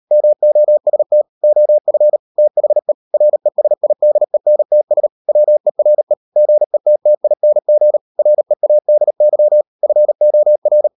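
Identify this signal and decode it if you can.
Morse code, heard as a keyed tone, about 38 wpm, 600 Hz. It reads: MOST OF THE RESIDENTS WERE GETTING READY FOR
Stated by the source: Morse code practice at 38 WPM, sending Ken Follett's novel Never in English, from CW med Gnister.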